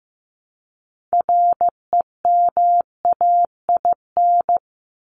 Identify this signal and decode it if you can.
Morse code, heard as a keyed tone, about 15 wpm, 700 Hz. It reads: REMAIN
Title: ES:S Suffix - Top 100 Words with Base Words 15wpm